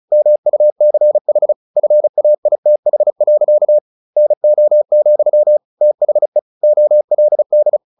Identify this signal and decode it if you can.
Morse code, heard as a keyed tone, about 34 wpm, 600 Hz. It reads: MUCH FAITH. NO, THE OLD
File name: the old man and the sea / the old man and the sea 1 35wpm